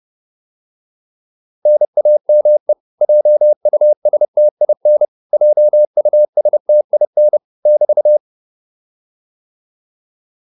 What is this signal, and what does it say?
Morse code, heard as a keyed tone, about 30 wpm, 600 Hz. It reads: NAME JUSTIN JUSTIN <BT>